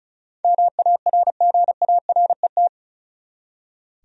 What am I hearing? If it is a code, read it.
Morse code, heard as a keyed tone, about 36 wpm, 700 Hz. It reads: MARGARET
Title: USA Common Names - 35wpm